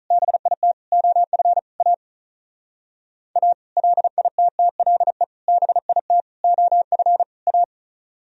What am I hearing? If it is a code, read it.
Morse code, heard as a keyed tone, about 39 wpm, 700 Hz. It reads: BIT OF A A LITTLE BIT OF A
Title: Speed-Racing - Sets of 5 Words 35wpm